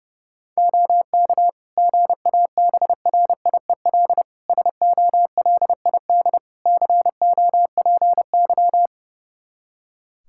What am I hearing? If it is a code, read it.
Morse code, heard as a keyed tone, about 30 wpm, 700 Hz. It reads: OK GABRIEL SOLID COPY
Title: Practice QSO - Exchange 3 - More Info - 30wpm